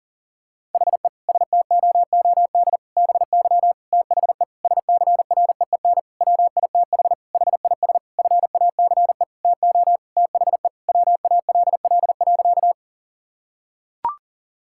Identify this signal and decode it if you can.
Morse code, heard as a keyed tone, about 40 wpm, 700 Hz. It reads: HE STOOD BY THE SCREEN WITH HIS FACE TO THE WALL.
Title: Sentences from Top 1000 Words - Review Part 6 - 40wpm